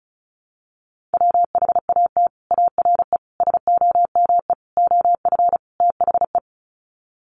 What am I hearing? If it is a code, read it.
Morse code, heard as a keyed tone, about 34 wpm, 700 Hz. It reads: WHAT ARE SOME OF THE
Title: Sets of 5 Words Encore 35wpm